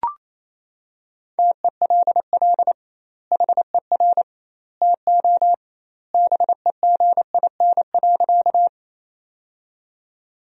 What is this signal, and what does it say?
Morse code, heard as a keyed tone, about 28 wpm, 700 Hz. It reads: TELL HER TO BEGIN.
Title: 2x Word-Spacing - Sentences from Top 300 Words - 28wpm